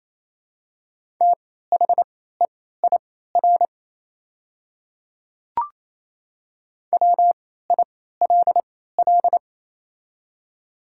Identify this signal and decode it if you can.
Morse code, heard as a keyed tone, about 28 wpm, spaced as for 16 wpm, 700 Hz. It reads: THEIR WILL
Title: 3x Character Spacing - Top 100 Words - 28wpm